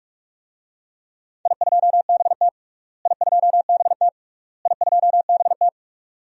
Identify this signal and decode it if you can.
Morse code, heard as a keyed tone, about 44 wpm, 700 Hz. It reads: I2BT I2BT I2BT
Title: DXCC-100 Call Signs - 4 Characters - Repeated 3x - 45wpm